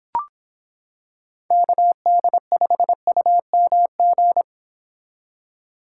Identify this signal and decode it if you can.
Morse code, heard as a keyed tone, about 26 wpm, 700 Hz. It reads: KD5UMG